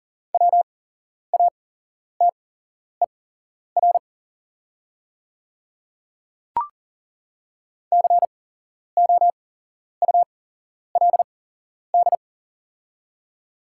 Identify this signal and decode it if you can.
Morse code, heard as a keyed tone, about 39 wpm, spaced as for 11 wpm, 690 Hz. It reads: WATER COULD
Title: 8x Character Spacing - Top 100 Words - 40wpm